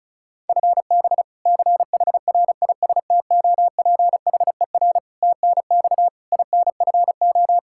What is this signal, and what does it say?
Morse code, heard as a keyed tone, about 34 wpm, 700 Hz. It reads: FB CHRISTOPHER TNX INFO
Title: K1USN SST QSO Practice 35wpm